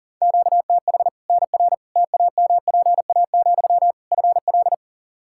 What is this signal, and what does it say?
Morse code, heard as a keyed tone, about 40 wpm, 700 Hz. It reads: QTH NR TAMPA, FL
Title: Practice QSO - Exchange 1 - Beginning - 40wpm